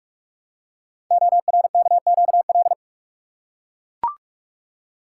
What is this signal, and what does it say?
Morse code, heard as a keyed tone, about 44 wpm, 700 Hz. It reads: ORKQL
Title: Code Groups - 5 Letters - 45wpm